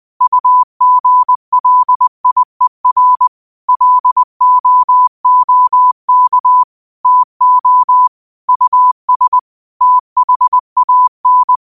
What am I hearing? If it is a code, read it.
Morse code, heard as a keyed tone, about 20 wpm, 1000 Hz. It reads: UGLIER LOOK TO US THAN